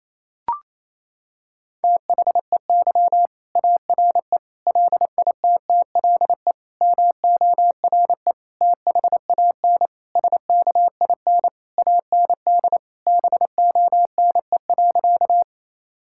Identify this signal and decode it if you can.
Morse code, heard as a keyed tone, about 28 wpm, 700 Hz. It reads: THEY ARE LITTLE MORE THAN SKIN AND BONE.
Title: Sentences from Top 1600 Words 28wpm